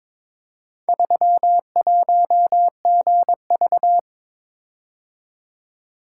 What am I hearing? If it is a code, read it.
Morse code, heard as a keyed tone, about 22 wpm, 700 Hz. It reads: 31GV
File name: practice / 22wpm / Four Letter-Number Combinations - 22wpm